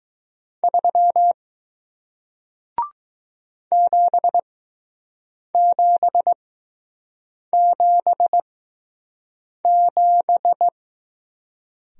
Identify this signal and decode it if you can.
Morse code, heard as a keyed tone, about 20 wpm, spaced as for 6 wpm, 700 Hz. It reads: 3 7777